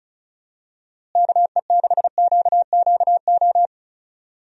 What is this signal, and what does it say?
Morse code, heard as a keyed tone, about 34 wpm, 700 Hz. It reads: KE6QQO
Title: USA Call Signs - 6 Characters - Repeated 3x - 35wpm